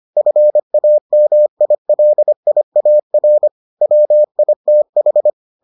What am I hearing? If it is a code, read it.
Morse code, heard as a keyed tone, about 25 wpm, 600 Hz. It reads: FAMILIAR WITH